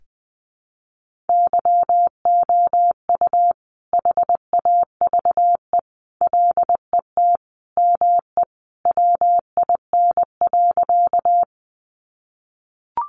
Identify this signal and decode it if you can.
Morse code, heard as a keyed tone, about 20 wpm, 700 Hz. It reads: YOU HAVE LET ME WIN.
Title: Sentences from Top 1500 Words 20wpm